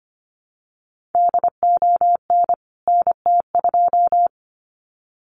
Morse code, heard as a keyed tone, about 25 wpm, 700 Hz. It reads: DON NT2
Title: CWT Exchange 25wpm